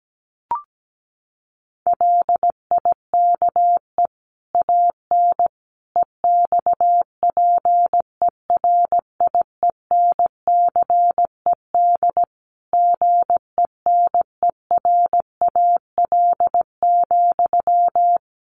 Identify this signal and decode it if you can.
Morse code, heard as a keyed tone, about 17 wpm, 700 Hz. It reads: LIKE AN EXPERIENCED GENERAL,
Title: Sentences from Top 1500 Words 17wpm